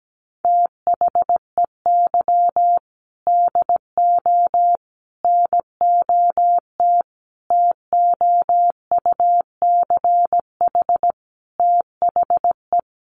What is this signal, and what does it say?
Morse code, heard as a keyed tone, about 17 wpm, 700 Hz. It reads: THEY DO NOT TOUCH THE
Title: Sentences from Top 1200 Words 17wpm